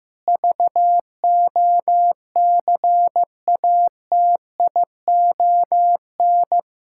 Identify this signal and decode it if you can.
Morse code, heard as a keyed tone, about 16 wpm, 700 Hz. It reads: VOCATION
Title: Pangram Challenge v2 - 15wpm